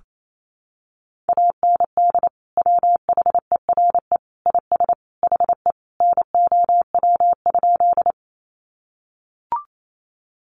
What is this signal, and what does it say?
Morse code, heard as a keyed tone, about 28 wpm, 700 Hz. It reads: AND WHERE IS HE NOW?
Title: Sentences from Top 200 Words 28wpm